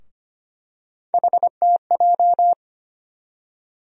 Morse code, heard as a keyed tone, about 25 wpm, 700 Hz. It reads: HTJ
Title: USA Call Sign Postfixes - 3 Letters 25wpm